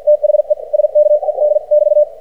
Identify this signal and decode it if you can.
Morse code, heard as a keyed tone, about 53 wpm, spaced as for 45 wpm, 600 Hz. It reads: THE SKED <BT>
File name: archived / CRSnet